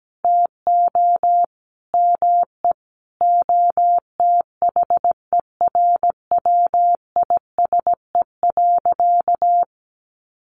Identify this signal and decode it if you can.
Morse code, heard as a keyed tone, about 17 wpm, 700 Hz. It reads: TO ME OTHERWISE.
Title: Sentences from Top 1200 Words 17wpm